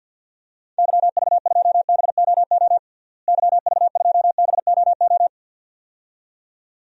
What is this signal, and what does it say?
Morse code, heard as a keyed tone, about 47 wpm, 700 Hz. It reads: YV2BOO YV2BOO
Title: DXCC-100 Call Signs - 6 Characters - Repeated 2x - 50wpm